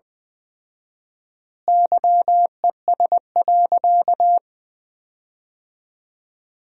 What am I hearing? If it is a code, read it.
Morse code, heard as a keyed tone, about 20 wpm, 700 Hz. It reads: YES.
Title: 4x Word-Spacing - Sentences from Top 500 Words - 20wpm